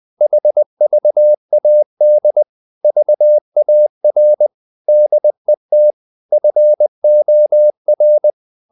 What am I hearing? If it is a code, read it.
Morse code, heard as a keyed tone, about 20 wpm, 600 Hz. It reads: HVAD VAR DET FOR
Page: Ferie 20 wpm